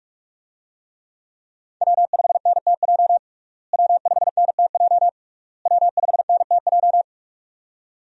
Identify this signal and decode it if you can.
Morse code, heard as a keyed tone, about 46 wpm, 700 Hz. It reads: W5NTJ W5NTJ W5NTJ
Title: USA Call Signs - 5 Characters - Repeated 3x - 45wpm